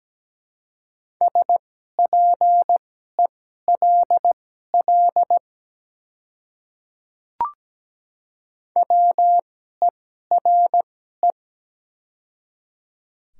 Morse code, heard as a keyed tone, about 17 wpm, spaced as for 12 wpm, 700 Hz. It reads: SPELL WERE
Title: 2x Character Spacing - Top 200 Words - 17wpm